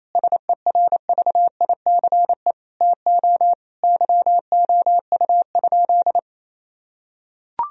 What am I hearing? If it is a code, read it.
Morse code, heard as a keyed tone, about 28 wpm, 700 Hz. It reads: SERVICE TO YOU?
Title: Sentences from Top 1000 Words - Review Part 4 - 28wpm